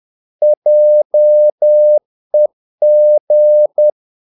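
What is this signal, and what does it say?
Morse code, heard as a keyed tone, about 10 wpm, 600 Hz. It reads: JEG
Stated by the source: Morse practice of short Danish words at 10 wpm, from CW med Gnister.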